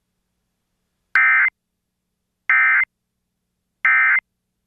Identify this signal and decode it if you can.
Emergency Alert System (SAME) tones, heard as harsh buzzing bursts. It is heard from 1.1 to 4.2 s.